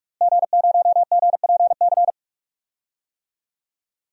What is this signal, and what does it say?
Morse code, heard as a keyed tone, about 46 wpm, 700 Hz. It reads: G0GPC
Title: DXCC-100 Call Signs - 5 Characters - Repeated 3x - 45wpm